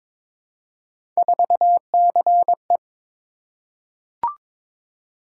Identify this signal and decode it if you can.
Morse code, heard as a keyed tone, about 23 wpm, 700 Hz. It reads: 4CE
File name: practice / 22wpm / Three Letter-Number Combinations - 22wpm